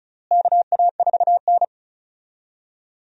Morse code, heard as a keyed tone, about 36 wpm, 700 Hz. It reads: KA4N
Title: USA Call Signs - 4 Characters - Repeated 2x - 35wpm